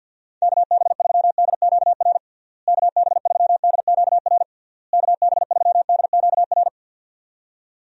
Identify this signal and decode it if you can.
Morse code, heard as a keyed tone, about 53 wpm, 700 Hz. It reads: KB3DQR KB3DQR KB3DQR
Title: USA Call Signs - 6 Characters - Repeated 3x - 50wpm